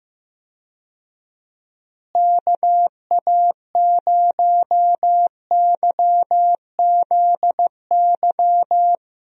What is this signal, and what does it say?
Morse code, heard as a keyed tone, about 15 wpm, 700 Hz. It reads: KA0YZY
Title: USA Call Signs - 6 Characters - Repeated 3x - 15wpm